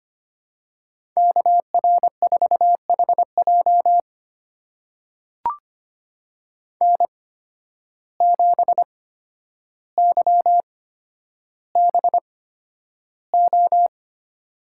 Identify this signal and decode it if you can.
Morse code, heard as a keyed tone, about 25 wpm, spaced as for 11 wpm, 700 Hz. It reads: KR4HJ N7YBO
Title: USA Call Signs - 5 Characters - 8x Character Spacing - 25wpm